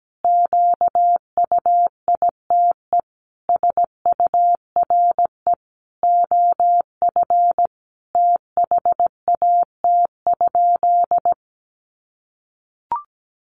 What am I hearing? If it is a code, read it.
Morse code, heard as a keyed tone, about 17 wpm, 700 Hz. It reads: QUITE SURE OF THAT?
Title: Sentences from Top 700 Words 17wpm